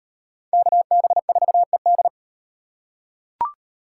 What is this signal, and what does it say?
Morse code, heard as a keyed tone, about 36 wpm, 700 Hz. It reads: KB4ED